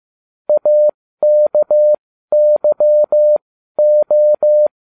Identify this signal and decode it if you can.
Morse code, heard as a keyed tone, about 15 wpm, spaced as for 12 wpm, 600 Hz. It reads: AKYO